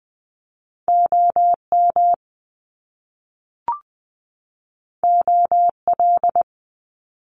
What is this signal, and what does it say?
Morse code, heard as a keyed tone, about 19 wpm, 700 Hz. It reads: OM OL
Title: Top Two Letter Combinations in Words 20wpm